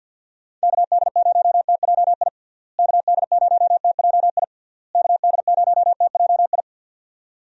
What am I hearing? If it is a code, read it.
Morse code, heard as a keyed tone, about 52 wpm, 700 Hz. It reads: KD0TJI KD0TJI KD0TJI
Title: USA Call Signs - 6 Characters - Repeated 3x - 50wpm